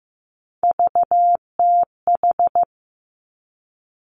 Morse code, heard as a keyed tone, about 15 wpm, 700 Hz. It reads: VTH
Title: Three Challenging Letter-Number Combinations 15wpm